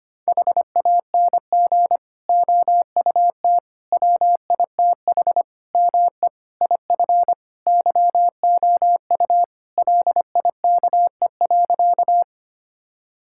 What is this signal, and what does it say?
Morse code, heard as a keyed tone, about 25 wpm, 700 Hz. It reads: HANG OUT WITH ME IF YOU LIKE.
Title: Sentences from Top 1600 Words 25wpm